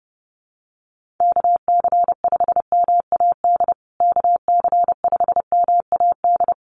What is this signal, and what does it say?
Morse code, heard as a keyed tone, about 28 wpm, 700 Hz. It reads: KC5MAD KC5MAD
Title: USA Call Signs Repeated 3x 30wpm